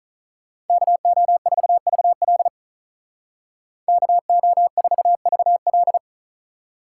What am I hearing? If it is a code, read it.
Morse code, heard as a keyed tone, about 43 wpm, 700 Hz. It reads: KO4VL KO4VL
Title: Speed Racing - USA 2x2 Call Signs 35wpm